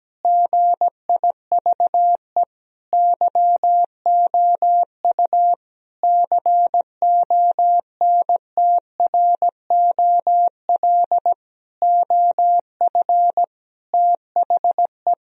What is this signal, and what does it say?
Morse code, heard as a keyed tone, about 17 wpm, 700 Hz. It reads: GIVE YOU CONTROL OF THE
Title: Sentences from Top 1000 Words - Review Part 3 - 17wpm